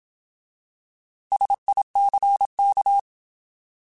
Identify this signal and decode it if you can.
Morse code, heard as a keyed tone, about 27 wpm, 800 Hz. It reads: SICK
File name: Alphabetical / Alphabetical mp3s / Ss @25